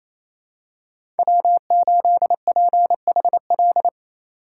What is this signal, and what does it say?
Morse code, heard as a keyed tone, about 28 wpm, 700 Hz. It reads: W8PHL
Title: USA Call Signs - 5 Characters - 2x Character Spacing - 28wpm